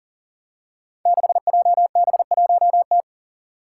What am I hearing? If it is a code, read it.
Morse code, heard as a keyed tone, about 42 wpm, 700 Hz. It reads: 6JB1T
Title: Five Challenging Letter-Number Combinations 40wpm